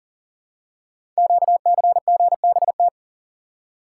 Morse code, heard as a keyed tone, about 42 wpm, 700 Hz. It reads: QCGBT
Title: Code Groups - 5 Letter:Numbers - 40wpm